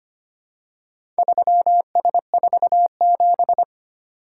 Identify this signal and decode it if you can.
Morse code, heard as a keyed tone, about 25 wpm, 700 Hz. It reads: 3S47
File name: practice / 25wpm / Four Letter-Number Combinations - 25wpm